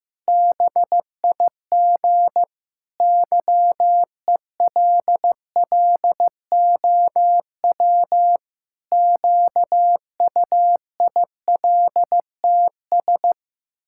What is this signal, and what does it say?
Morse code, heard as a keyed tone, about 15 wpm, 700 Hz. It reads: BIG YELLOW QUILTS
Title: Pangram Challenge v2 - 15wpm